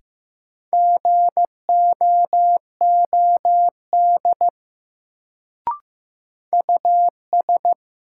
Morse code, heard as a keyed tone, about 15 wpm, 700 Hz. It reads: GOOD US
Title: Sets of 2 Words - New Word Drill - 15wpm